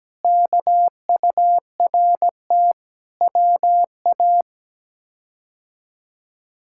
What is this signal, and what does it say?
Morse code, heard as a keyed tone, about 17 wpm, 700 Hz. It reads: KURT WA
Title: K1USN SST QSO Practice - 17wpm